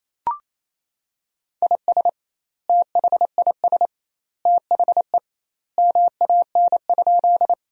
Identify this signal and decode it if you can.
Morse code, heard as a keyed tone, about 28 wpm, 700 Hz. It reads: IS THIS THE MAN?
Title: 2x Word-Spacing - Sentences from Top 200 Words - 28wpm